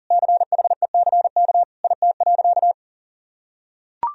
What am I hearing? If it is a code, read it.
Morse code, heard as a keyed tone, about 37 wpm, 700 Hz. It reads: CHECK IT.
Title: Sentences from Top 1000 Words - Review Part 3 - 40wpm